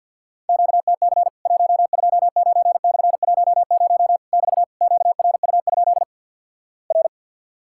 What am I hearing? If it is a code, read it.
Morse code, heard as a keyed tone, about 52 wpm, 700 Hz. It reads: QTC 129/10 <BT> QRU? R
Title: Worked All Europe DX Contest - QTC Exchange - 50wpm